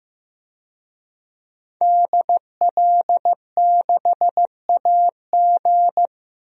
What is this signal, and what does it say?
Morse code, heard as a keyed tone, about 15 wpm, 700 Hz. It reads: DL6AG
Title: DXCC-100 Call Signs - 5 Characters - 15wpm